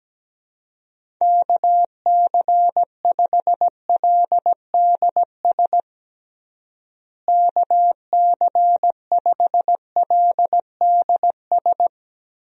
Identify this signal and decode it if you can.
Morse code, heard as a keyed tone, about 17 wpm, 700 Hz. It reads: KC5LDS KC5LDS
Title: USA Call Signs - 6 Characters - Repeated 3x - 17wpm